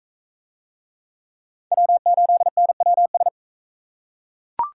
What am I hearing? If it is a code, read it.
Morse code, heard as a keyed tone, about 37 wpm, 700 Hz. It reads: W8NWS